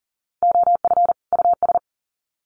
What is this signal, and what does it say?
Morse code, heard as a keyed tone, about 40 wpm, 700 Hz. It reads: OF US
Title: Sets of 2 Words 40wpm